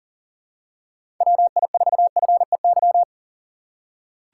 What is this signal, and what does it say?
Morse code, heard as a keyed tone, about 40 wpm, 700 Hz. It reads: WI4FEY
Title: USA Call Signs - 6 Characters - 6x Character Spacing - 40wpm